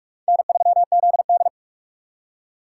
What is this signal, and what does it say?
Morse code, heard as a keyed tone, about 43 wpm, 700 Hz. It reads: N3ZD
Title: USA Call Signs - 3x Character Spacing - 45wpm